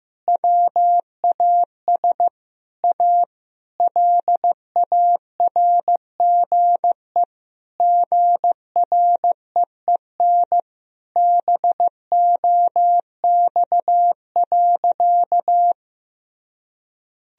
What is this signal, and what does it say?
Morse code, heard as a keyed tone, about 15 wpm, 700 Hz. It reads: WAS A LARGE GREEN BOX.
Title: Sentences from Top 1000 Words - Review Part 2 - 15wpm